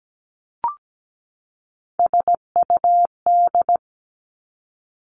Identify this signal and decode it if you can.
Morse code, heard as a keyed tone, about 17 wpm, 700 Hz. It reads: SUD
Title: Three Challenging Letter-Number Combinations 17wpm